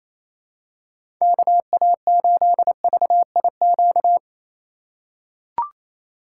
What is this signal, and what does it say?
Morse code, heard as a keyed tone, about 28 wpm, 700 Hz. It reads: KA8VIQ
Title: USA Call Signs - 8x Character Spacing - 28wpm